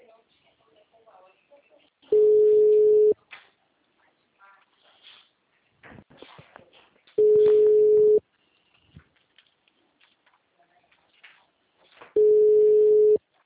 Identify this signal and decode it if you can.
Ringback tone (Europe, 425 Hz), with speech.
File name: out- -2033- - - .66700